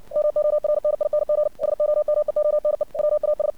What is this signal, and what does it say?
Morse code, heard as a keyed tone, about 57 wpm, 600 Hz. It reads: MORNING SOMEONE WAS
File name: archived / CRSnet